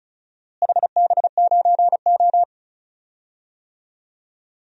Morse code, heard as a keyed tone, about 34 wpm, 700 Hz. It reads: HB9O
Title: DXCC-100 Call Signs - 4 Characters - Repeated 2x - 35wpm